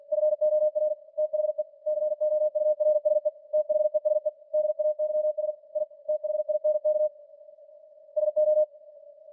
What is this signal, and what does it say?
Morse code, heard as a keyed tone, about 48 wpm, 610 Hz. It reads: FOR THE COURSE THESE DAYS I THINK SO